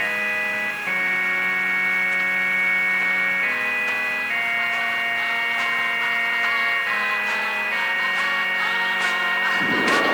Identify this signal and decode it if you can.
Dial-up modem handshake.